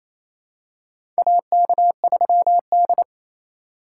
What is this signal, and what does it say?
Morse code, heard as a keyed tone, about 27 wpm, 700 Hz. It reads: AK3D
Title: USA Call Signs - 4 Characters - 4x Character Spacing - 28wpm